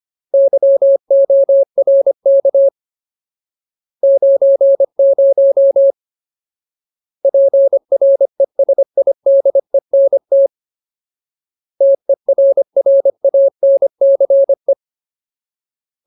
Morse code, heard as a keyed tone, about 25 wpm, 550 Hz. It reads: YORK 90 PRESIDENT TERRANCE
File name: Ultimate Word List / 25WPM Ultimate